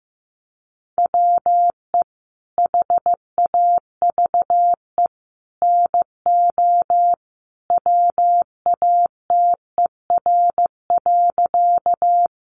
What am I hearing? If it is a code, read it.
Morse code, heard as a keyed tone, about 15 wpm, 700 Hz. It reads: WE HAVE NO WATER.
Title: Sentences from Top 100 Words 15wpm